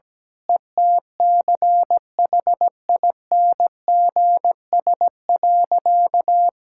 Morse code, heard as a keyed tone, about 17 wpm, 700 Hz. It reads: ETCHINGS.
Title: Pangram Challenge v2 - 17wpm